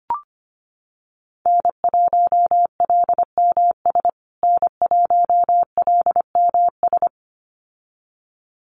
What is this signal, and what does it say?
Morse code, heard as a keyed tone, about 25 wpm, 700 Hz. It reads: N1LMS N1LMS
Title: USA Call Signs Repeated 2x 25wpm